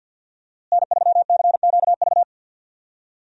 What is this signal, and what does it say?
Morse code, heard as a keyed tone, about 46 wpm, 700 Hz. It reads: N3CQV
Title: USA Call Signs - 5 Characters - Repeated 2x - 50wpm